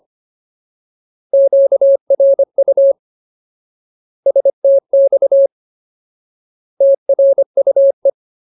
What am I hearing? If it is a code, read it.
Morse code, heard as a keyed tone, about 25 wpm, 550 Hz. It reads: QRU STX TRUE
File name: Ultimate Word List / 25WPM Ultimate